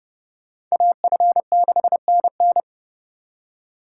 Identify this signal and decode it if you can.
Morse code, heard as a keyed tone, about 36 wpm, 700 Hz. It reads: AF6NN
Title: Speed Racing - USA Call Signs 30wpm